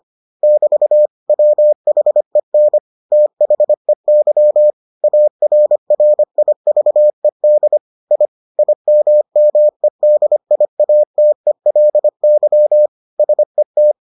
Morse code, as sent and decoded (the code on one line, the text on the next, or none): -...- / .-- .... -. / - .... -.-- / .- .-. .-. .. ...- . -.. / .. / .. -- -- . -.. .. .- - . .-.. -.-- / ... -
<BT> WHEN THEY ARRIVED I IMMEDIATELY SET